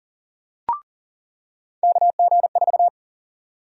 Morse code, heard as a keyed tone, about 40 wpm, 700 Hz. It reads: KG4
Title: USA Call Sign Prefixes - 40wpm